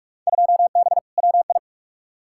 Morse code, heard as a keyed tone, about 43 wpm, 700 Hz. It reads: 2B WI
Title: Field Day Exchanges - 45wpm